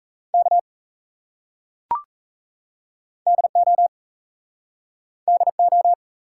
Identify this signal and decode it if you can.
Morse code, heard as a keyed tone, about 38 wpm, 700 Hz. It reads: K DO DO